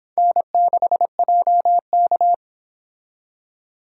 Morse code, heard as a keyed tone, about 25 wpm, 700 Hz. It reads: N6JK